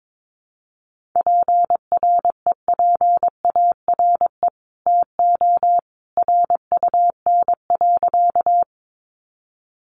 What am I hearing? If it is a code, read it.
Morse code, heard as a keyed tone, about 22 wpm, 700 Hz. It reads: PREPARE TO RUN.
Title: Sentences from Top 900 Words 22wpm